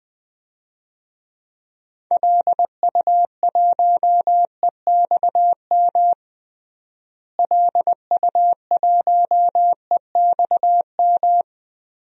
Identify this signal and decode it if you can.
Morse code, heard as a keyed tone, about 20 wpm, 700 Hz. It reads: LU1EXM LU1EXM
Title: DXCC-100 Call Signs - 6 Characters - Repeated 3x - 20wpm